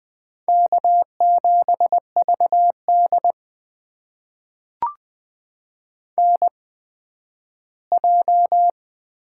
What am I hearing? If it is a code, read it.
Morse code, heard as a keyed tone, about 20 wpm, spaced as for 12 wpm, 700 Hz. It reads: K7VD NJ
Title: USA Call Signs - 4 Characters - 8x Character Spacing - 20wpm